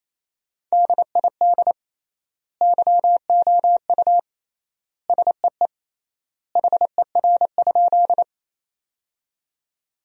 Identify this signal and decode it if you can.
Morse code, heard as a keyed tone, about 28 wpm, 700 Hz. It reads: DID YOU SEE HER?
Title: 3x Word-Spacing - Sentences from Top 100 Words - 28wpm